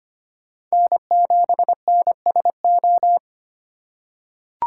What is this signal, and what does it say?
Morse code, heard as a keyed tone, about 25 wpm, 700 Hz. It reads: N7NSO
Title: USA Call Signs - 5 Characters - 2x Character Spacing - 25wpm